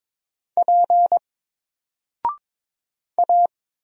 Morse code, heard as a keyed tone, about 22 wpm, 700 Hz. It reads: P A
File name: Single-Letter-Number-22wpm